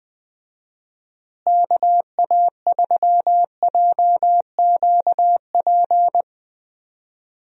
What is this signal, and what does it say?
Morse code, heard as a keyed tone, about 21 wpm, 700 Hz. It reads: KA3JQP
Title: USA Call Signs - 6 Characters - Repeated 3x - 20wpm